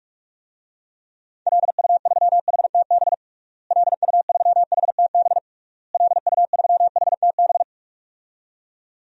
Morse code, heard as a keyed tone, about 46 wpm, 700 Hz. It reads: LU3HTB LU3HTB LU3HTB
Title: DXCC-100 Call Signs - 6 Characters - Repeated 3x - 45wpm